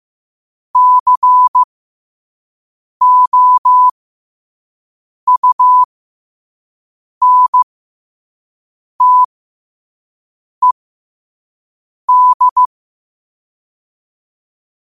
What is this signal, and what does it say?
Morse code, heard as a keyed tone, about 15 wpm, spaced as for 5 wpm, 1000 Hz. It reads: COUNTED
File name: qotdfiles / 5wpm